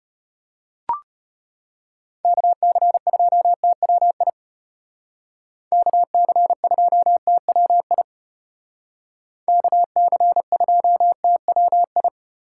Morse code, heard as a keyed tone, about 33 wpm, 700 Hz. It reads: KC2TWI KC2TWI KC2TWI